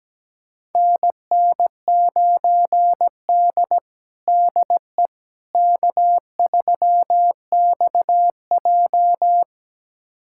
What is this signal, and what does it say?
Morse code, heard as a keyed tone, about 17 wpm, 700 Hz. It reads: NN9D DE K3XJ
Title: USA Callsign DE Callsign - 4 Characters - 17wpm